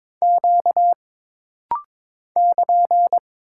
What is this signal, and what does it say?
Morse code, heard as a keyed tone, about 22 wpm, 700 Hz. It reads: Q <KN>
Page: This rapid-fire timing is geared to push your ICR (Instant Character Recognition) skills.